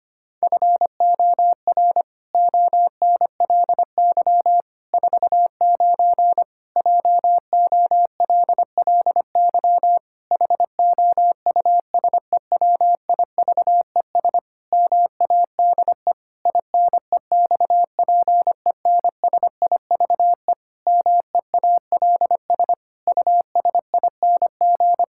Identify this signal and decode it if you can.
Morse code, heard as a keyed tone, about 25 wpm, 700 Hz. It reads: FOR ONLY 49 JOLLY HOUSEWIVES MADE INEXPENSIVE MEALS USING
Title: Pangram Challenge v2 - 25wpm